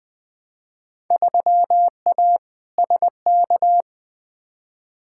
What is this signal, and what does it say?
Morse code, heard as a keyed tone, about 20 wpm, 700 Hz. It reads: 3A SK
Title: Field Day Exchanges Encore - 20wpm